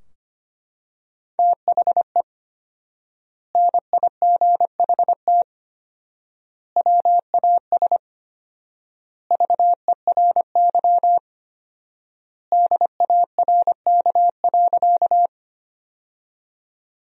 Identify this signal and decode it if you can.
Morse code, heard as a keyed tone, about 25 wpm, 700 Hz. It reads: THE NIGHT WAS VERY DARK.
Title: 4x Word-Spacing - Sentences from Top 500 Words - 25wpm